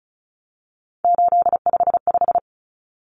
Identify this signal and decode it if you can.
Morse code, heard as a keyed tone, about 34 wpm, 700 Hz. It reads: 855
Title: Numbers - 3-Digit 35wpm